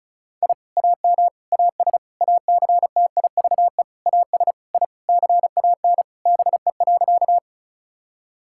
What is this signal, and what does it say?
Morse code, heard as a keyed tone, about 36 wpm, 700 Hz. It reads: I AM AS ACTIVE AS I CAN BE.
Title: Sentences from Top 1000 Words - Review Part 6 - 35wpm